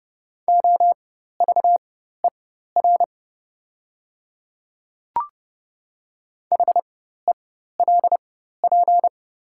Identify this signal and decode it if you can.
Morse code, heard as a keyed tone, about 30 wpm, spaced as for 14 wpm, 700 Hz. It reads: OVER HELP
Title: 4x Character Spacing - Top 300 Words - 30wpm